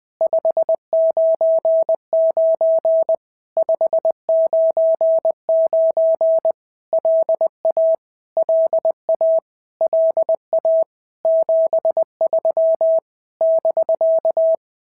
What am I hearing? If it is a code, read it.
Morse code, heard as a keyed tone, about 20 wpm, 650 Hz. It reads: 599 599 LA LA LA 73 <BK>
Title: POTA QSO Practice - 20wpm